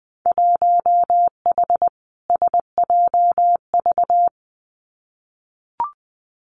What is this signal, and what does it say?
Morse code, heard as a keyed tone, about 20 wpm, 700 Hz. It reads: 1H SJV
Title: Winter Field Day Exchanges 20wpm